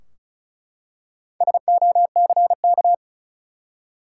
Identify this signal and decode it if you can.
Morse code, heard as a keyed tone, about 34 wpm, 700 Hz. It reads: SOCK
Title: Four Letter Words - 35wpm